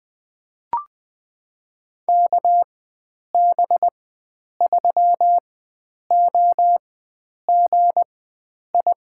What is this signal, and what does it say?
Morse code, heard as a keyed tone, about 20 wpm, spaced as for 9 wpm, 700 Hz. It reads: KB3OGI
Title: USA Call Signs - 6 Characters - 4x Character Spacing - 20wpm